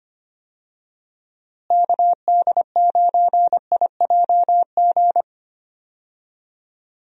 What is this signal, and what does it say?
Morse code, heard as a keyed tone, about 25 wpm, 700 Hz. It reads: KD9IJG
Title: USA Call Signs - 6 Characters - Repeated 3x - 25wpm